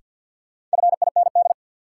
Morse code, heard as a keyed tone, about 53 wpm, 700 Hz. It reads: FIND
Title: Sets of 2 Words - New Word Drill - 50wpm